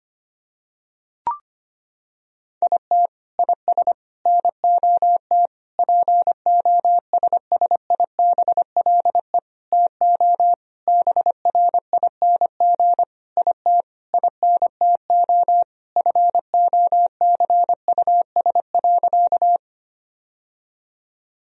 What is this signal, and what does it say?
Morse code, heard as a keyed tone, about 25 wpm, 700 Hz. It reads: IT IS NOT POSSIBLE TO BRING IT INTO FOCUS.
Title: Sentences from Top 1000 Words - Review Part 4 - 25wpm